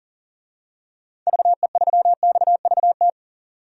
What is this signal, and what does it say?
Morse code, heard as a keyed tone, about 40 wpm, 700 Hz. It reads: VE3XVT
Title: DXCC-100 Call Signs - 6 Characters - 40wpm